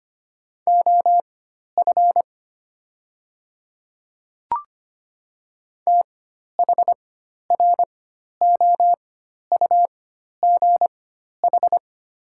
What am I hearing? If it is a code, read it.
Morse code, heard as a keyed tone, about 25 wpm, spaced as for 12 wpm, 690 Hz. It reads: OF THROUGH
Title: 4x Character Spacing - Top 200 Words - 25wpm